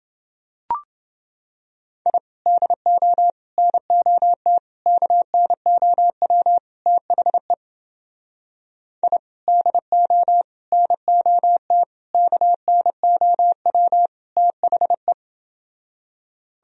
Morse code, heard as a keyed tone, about 29 wpm, 700 Hz. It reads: I DO NOT KNOW THE I DO NOT KNOW THE